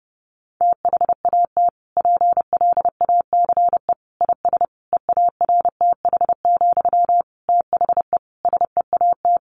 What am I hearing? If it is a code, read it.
Morse code, heard as a keyed tone, about 30 wpm, 700 Hz. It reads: THAT PLACE IS EARTH, THE SEAT
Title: Sentences from Top 1500 Words 30wpm